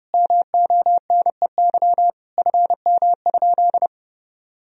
Morse code, heard as a keyed tone, about 32 wpm, 700 Hz. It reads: MONEY FM?
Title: Sentences from Top 400 Words and Abbreviations 30wpm